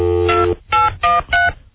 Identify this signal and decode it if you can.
DTMF, heard as telephone touch tones, tones of about 180 ms. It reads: 2813